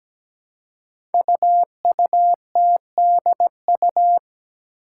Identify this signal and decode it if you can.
Morse code, heard as a keyed tone, about 17 wpm, 700 Hz. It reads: UUTDU
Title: Code Groups - 5 Letters - 17wpm